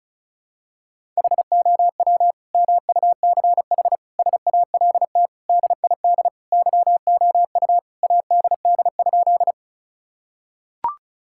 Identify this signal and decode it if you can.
Morse code, heard as a keyed tone, about 34 wpm, 700 Hz. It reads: HOW MUCH SALT DID YOU ADD?
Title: Sentences from Top 1000 Words - Review Part 5 - 35wpm